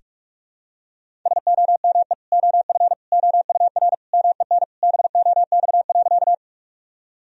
Morse code, heard as a keyed tone, about 41 wpm, 700 Hz. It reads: SOME OF OUR MEN BOX.
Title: Sentences from Top 1000 Words - Review Part 2 - 45wpm